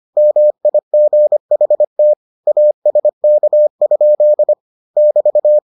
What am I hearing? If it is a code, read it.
Morse code, heard as a keyed tone, about 25 wpm, 600 Hz. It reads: MIGHT ASK? <BT>